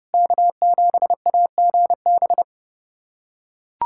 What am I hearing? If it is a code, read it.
Morse code, heard as a keyed tone, about 30 wpm, 700 Hz. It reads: K7AGB
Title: USA Callsign DE Callsign 30wpm